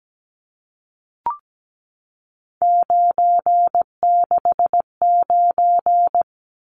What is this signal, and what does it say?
Morse code, heard as a keyed tone, about 17 wpm, 700 Hz. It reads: 969
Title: Numbers - 3-Digit 17wpm